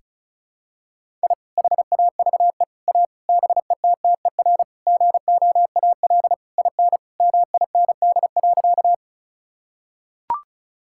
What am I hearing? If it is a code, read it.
Morse code, heard as a keyed tone, about 34 wpm, 700 Hz. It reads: I HAVE A BETTER GOAL IN MIND.
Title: Sentences from Top 1000 Words - Review Part 5 - 35wpm